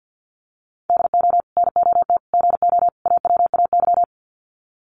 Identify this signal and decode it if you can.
Morse code, heard as a keyed tone, about 50 wpm, 700 Hz. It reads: DO NOT GO AWAY
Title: Sets of 4 Words 50wpm